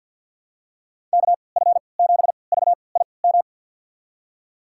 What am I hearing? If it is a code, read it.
Morse code, heard as a keyed tone, about 53 wpm, spaced as for 29 wpm, 700 Hz. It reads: KF7VIM
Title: USA Call Signs - 6 Characters - 3x Character Spacing - 50wpm